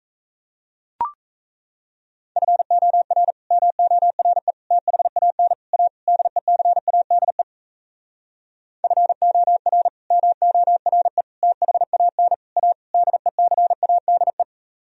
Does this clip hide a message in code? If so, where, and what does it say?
Yes: Morse code from 2.4 to 14.4 s, heard as a keyed tone, about 36 wpm, 700 Hz. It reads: FOR MORE THAN A DECADE FOR MORE THAN A DECADE